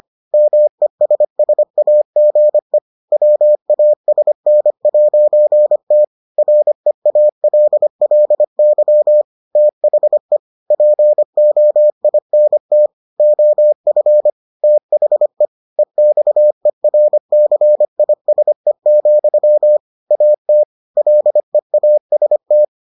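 Morse code, heard as a keyed tone, about 25 wpm, 600 Hz. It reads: MESSAGE WASN'T REALLY THE POINT OF THE EXERCISE, AT LEAST
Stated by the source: Morse edition of the Foundations of Amateur Radio episodes.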